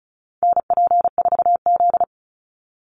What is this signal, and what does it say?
Morse code, heard as a keyed tone, about 34 wpm, 700 Hz. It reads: NP4Z
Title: Call Signs Repeated 2x - 35wpm